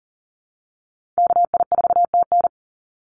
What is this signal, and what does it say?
Morse code, heard as a keyed tone, about 40 wpm, 700 Hz. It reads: KI4TN
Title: NA SOTA Activator Callsigns 40wpm